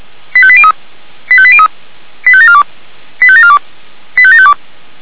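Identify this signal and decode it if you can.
Five-tone selective call (ZVEI), heard as quick runs of five tones, tones of about 70 ms. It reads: 85803 85803 85742 85742 85752
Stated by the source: ZVEI five-tone call received on an SDR, from the Feuerwehr Orlamünde fire brigade audio server.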